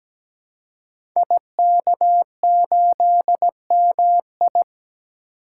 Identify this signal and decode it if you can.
Morse code, heard as a keyed tone, about 17 wpm, 700 Hz. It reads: IK8MI
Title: Code Groups - 5 Letter:Numbers - 17wpm